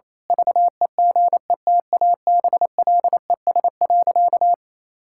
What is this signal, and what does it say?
Morse code, heard as a keyed tone, about 28 wpm, 700 Hz. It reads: VEGETABLES.
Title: Pangram Challenge v2 - 28wpm